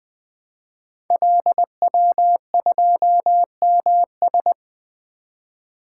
Morse code, heard as a keyed tone, about 20 wpm, 700 Hz. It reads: LW2MS
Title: Code Groups - 5 Letter:Numbers - 20wpm